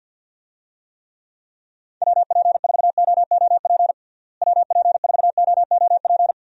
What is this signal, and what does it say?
Morse code, heard as a keyed tone, about 47 wpm, 700 Hz. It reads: WP4OOP WP4OOP
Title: USA Call Signs - 6 Characters - Repeated 3x - 50wpm